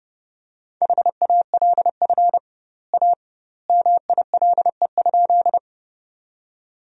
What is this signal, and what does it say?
Morse code, heard as a keyed tone, about 32 wpm, 700 Hz. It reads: HALF A MILE?
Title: 2x Word-Spacing - Sentences from Top 400 Words - 30wpm